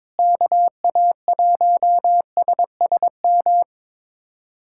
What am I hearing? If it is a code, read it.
Morse code, heard as a keyed tone, about 22 wpm, 700 Hz. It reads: KA1SSM
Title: USA Callsign DE Callsign - 6 Characters - 22wpm